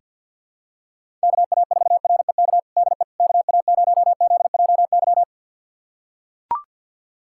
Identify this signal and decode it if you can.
Morse code, heard as a keyed tone, about 45 wpm, 700 Hz. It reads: KA4REK DE KA0ZJY
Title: USA Callsign DE Callsign 50wpm